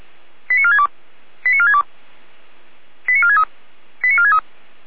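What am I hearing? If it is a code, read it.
Five-tone selective call (ZVEI), heard as quick runs of five tones, tones of about 70 ms. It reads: 89462 89462 89463 89463
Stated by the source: ZVEI five-tone call received by SDR.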